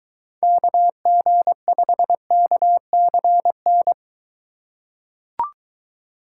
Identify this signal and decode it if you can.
Morse code, heard as a keyed tone, about 23 wpm, 700 Hz. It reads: KG5KCN